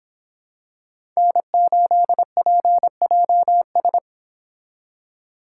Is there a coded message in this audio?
Yes: Morse code from 1.2 to 4.0 s, heard as a keyed tone, about 25 wpm, 700 Hz. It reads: N8PJS